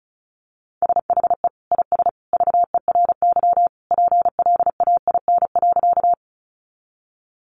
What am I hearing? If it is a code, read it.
Morse code, heard as a keyed tone, about 34 wpm, 700 Hz. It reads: SHE IS VERY PLAIN.
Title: Sentences from Top 400 Words 35wpm